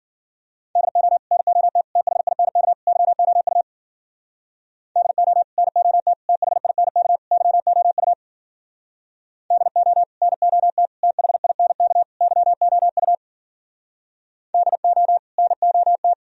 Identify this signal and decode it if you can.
Morse code, heard as a keyed tone, about 52 wpm, 700 Hz. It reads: DO NOT THINK YOU DO NOT THINK YOU DO NOT THINK YOU DO NOT